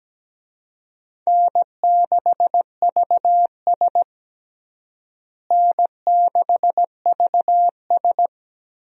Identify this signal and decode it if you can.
Morse code, heard as a keyed tone, about 17 wpm, 700 Hz. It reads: N6VS N6VS
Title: USA Call Signs - 4 Characters - Repeated 3x - 17wpm